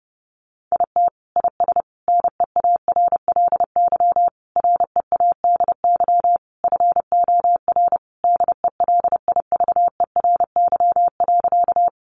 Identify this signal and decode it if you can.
Morse code, heard as a keyed tone, about 30 wpm, 700 Hz. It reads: IT IS NEARLY READY FOR DELIVERY.
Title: Sentences from Top 1500 Words 30wpm